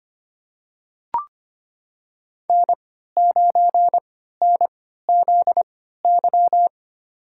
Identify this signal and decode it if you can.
Morse code, heard as a keyed tone, about 25 wpm, spaced as for 14 wpm, 700 Hz. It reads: N9NZY
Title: USA Call Signs - 5 Characters - 3x Character Spacing - 25wpm